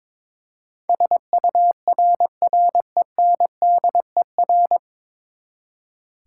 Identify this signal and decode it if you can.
Morse code, heard as a keyed tone, about 22 wpm, 700 Hz. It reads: SURRENDER
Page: Morse Code Ninja practice 22wpm